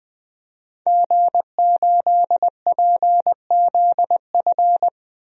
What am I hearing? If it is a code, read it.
Morse code, heard as a keyed tone, about 20 wpm, 700 Hz. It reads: G8PZF
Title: DXCC-100 Call Signs - 5 Characters - 20wpm